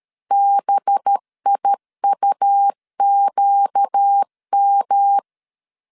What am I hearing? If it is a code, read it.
Morse code, heard as a keyed tone, about 13 wpm, 800 Hz. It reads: BIUQM
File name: CW / DET / 1 6 1500C S B 12wpm 3431